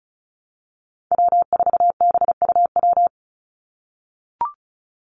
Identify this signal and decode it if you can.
Morse code, heard as a keyed tone, about 34 wpm, 700 Hz. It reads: W4BUW
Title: Call Signs - 35wpm